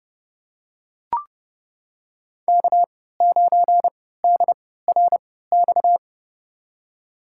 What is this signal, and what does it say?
Morse code, heard as a keyed tone, about 30 wpm, spaced as for 17 wpm, 700 Hz. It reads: K9DRX
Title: USA Call Signs - 3x Character Spacing - 30wpm